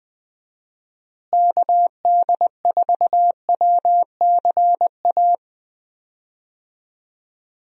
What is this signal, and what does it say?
Morse code, heard as a keyed tone, about 21 wpm, 700 Hz. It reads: KD4WCA
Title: USA Call Signs - 6 Characters - Repeated 2x - 20wpm